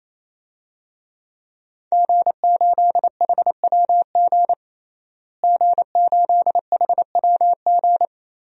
Morse code, heard as a keyed tone, about 28 wpm, 700 Hz. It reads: G8HWG G8HWG
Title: DXCC-100 Call Signs - 5 Characters - Repeated 3x - 28wpm